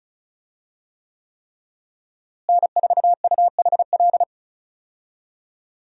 Morse code, heard as a keyed tone, about 36 wpm, 700 Hz. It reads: N4UHL
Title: USA Call Signs - 5 Characters - Repeated 3x - 35wpm